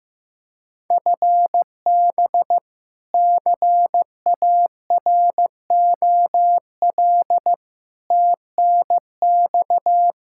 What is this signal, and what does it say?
Morse code, heard as a keyed tone, about 15 wpm, 700 Hz. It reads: FB CAROL TNX
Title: K1USN SST QSO Practice 15wpm